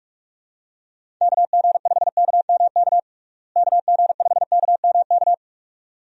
Morse code, heard as a keyed tone, about 46 wpm, 700 Hz. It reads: KG5KMK KG5KMK
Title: USA Call Signs - 6 Characters - Repeated 2x - 45wpm